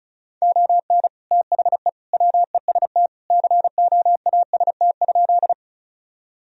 Morse code, heard as a keyed tone, about 34 wpm, 700 Hz. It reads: ON THE WEST COAST?
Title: Sentences from Top 1000 Words - Review Part 4 - 35wpm